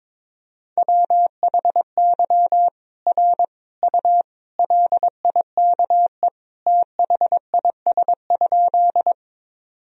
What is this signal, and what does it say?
Morse code, heard as a keyed tone, about 22 wpm, 700 Hz. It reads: WHY R U LIKE THIS?
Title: Sentences from Top 200 Words and Abbreviations 22wpm